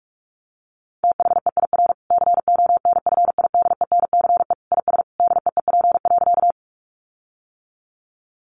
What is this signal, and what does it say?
Morse code, heard as a keyed tone, about 44 wpm, 700 Hz. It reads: THEIR CONFIDENCE IS DEEP.
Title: Sentences from Top 1500 Words 45wpm